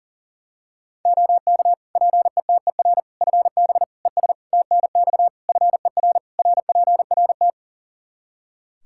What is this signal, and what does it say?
Morse code, heard as a keyed tone, about 40 wpm, 700 Hz. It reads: OK PETER FB ES TNX FER RPRT
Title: K1USN SST QSO Practice 40wpm